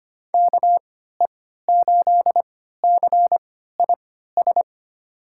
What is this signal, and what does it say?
Morse code, heard as a keyed tone, about 25 wpm, spaced as for 14 wpm, 700 Hz. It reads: KE8CIS
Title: USA Call Signs - 6 Characters - 3x Character Spacing - 25wpm